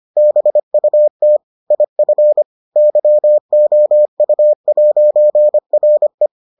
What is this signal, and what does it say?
Morse code, heard as a keyed tone, about 25 wpm, 600 Hz. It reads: BUT IF YOU'RE